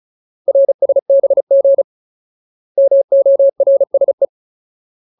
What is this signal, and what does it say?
Morse code, heard as a keyed tone, about 34 wpm, 550 Hz. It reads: RSBG MORSE
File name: Ultimate Word List / 35WPM Ultimate